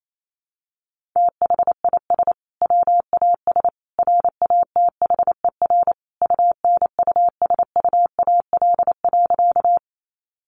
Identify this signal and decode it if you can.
Morse code, heard as a keyed tone, about 28 wpm, 700 Hz. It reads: THIS WAS RATHER UNUSUAL.
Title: Sentences from Top 1400 Words 28wpm